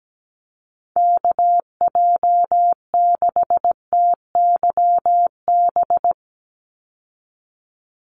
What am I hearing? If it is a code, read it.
Morse code, heard as a keyed tone, about 17 wpm, 700 Hz. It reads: KJ6TYB
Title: USA Call Signs 17wpm